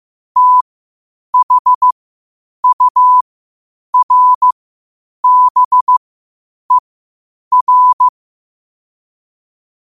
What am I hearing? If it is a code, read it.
Morse code, heard as a keyed tone, about 15 wpm, spaced as for 8 wpm, 1000 Hz. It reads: THURBER
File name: qotdfiles / 8wpm